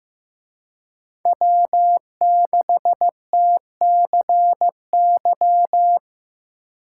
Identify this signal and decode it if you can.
Morse code, heard as a keyed tone, about 18 wpm, 700 Hz. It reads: W6TCY
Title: Speed Racing - USA 1x3 Call Signs 15wpm